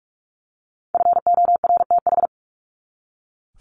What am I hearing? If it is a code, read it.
Morse code, heard as a keyed tone, about 44 wpm, 700 Hz. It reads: FORTH
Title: Top 1751-2000 Words 45wpm